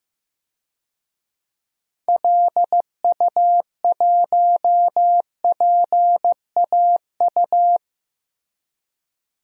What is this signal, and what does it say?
Morse code, heard as a keyed tone, about 15 wpm, 700 Hz. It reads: LU1PAU
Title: DXCC-100 Call Signs - 6 Characters - Repeated 2x - 15wpm